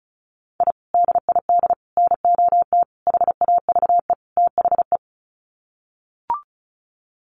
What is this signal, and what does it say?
Morse code, heard as a keyed tone, about 34 wpm, 700 Hz. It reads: I DID NOT HAVE THE
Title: Sets of 5 Words Encore 35wpm